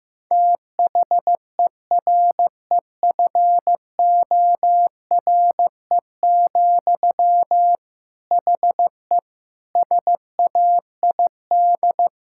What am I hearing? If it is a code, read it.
Morse code, heard as a keyed tone, about 15 wpm, 700 Hz. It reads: THEREFORE, HE SAID
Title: Sentences from Top 1000 Words - Review Part 5 - 15wpm